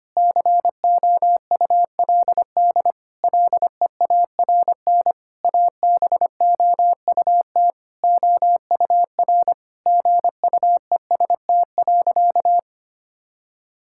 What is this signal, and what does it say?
Morse code, heard as a keyed tone, about 25 wpm, 700 Hz. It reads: COULD LEARN ABOUT OUR GUEST.
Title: Sentences from Top 1600 Words 25wpm